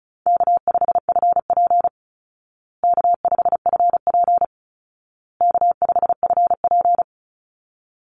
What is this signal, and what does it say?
Morse code, heard as a keyed tone, about 34 wpm, 700 Hz. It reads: K5FP K5FP K5FP
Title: Easy Call Signs Repeated 3x - 35wpm